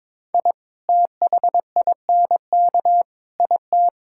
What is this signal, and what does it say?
Morse code, heard as a keyed tone, about 22 wpm, 700 Hz. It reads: I THINK IT
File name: practice / 22wpm / Sentences with 2-Word Phrases - 22wpm